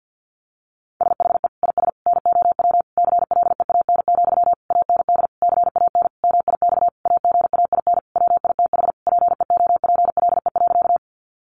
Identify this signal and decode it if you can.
Morse code, heard as a keyed tone, about 48 wpm, 700 Hz. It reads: SHE IS NOW CLEAN, AND CAN MIX AGAIN WITH PEOPLE.
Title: Sentences from Top 1100 Words 50wpm